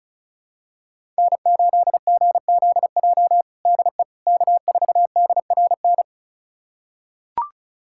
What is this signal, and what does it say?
Morse code, heard as a keyed tone, about 36 wpm, 700 Hz. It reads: N8GZJ DE K4DRN
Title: USA Callsign DE Callsign 35wpm